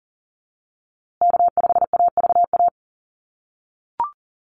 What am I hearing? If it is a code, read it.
Morse code, heard as a keyed tone, about 40 wpm, 700 Hz. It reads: K5AVA
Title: USA Call Signs Repeated 3x 40wpm